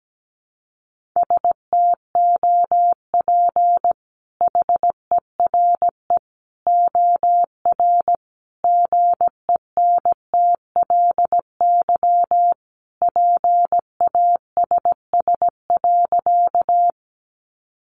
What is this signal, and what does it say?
Morse code, heard as a keyed tone, about 17 wpm, 700 Hz. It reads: STOP HERE OR GENTLY PASS.
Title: Sentences from Top 1500 Words 17wpm